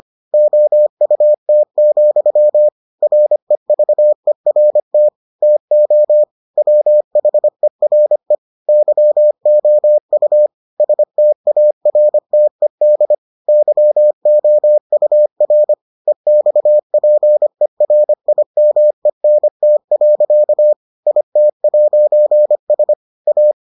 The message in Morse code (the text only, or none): OUT, REVERT TO WHERE YOU STARTED YOUR EXPERIMENT. IT'S A